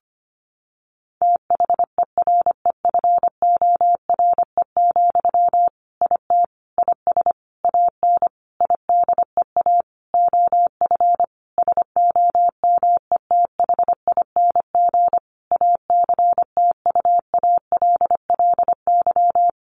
Morse code, heard as a keyed tone, about 25 wpm, 700 Hz. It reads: THEREFORE, IT IS AN IDEA OF SOMETHING ACTUALLY
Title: Sentences from Top 1500 Words 25wpm